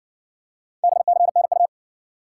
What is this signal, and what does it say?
Morse code, heard as a keyed tone, about 52 wpm, 700 Hz. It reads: BCNU